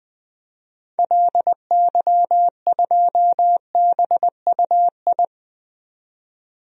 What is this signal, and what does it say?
Morse code, heard as a keyed tone, about 20 wpm, 700 Hz. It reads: LY2BUI
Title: DXCC-100 Call Signs - 6 Characters - 20wpm